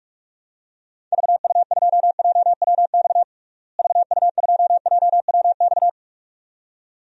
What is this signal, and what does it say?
Morse code, heard as a keyed tone, about 46 wpm, 700 Hz. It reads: VU2JWX VU2JWX
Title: DXCC-100 Call Signs - 6 Characters - Repeated 2x - 45wpm